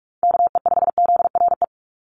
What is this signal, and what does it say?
Morse code, heard as a keyed tone, about 44 wpm, 700 Hz. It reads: KE5ZRE
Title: USA Call Signs 45wpm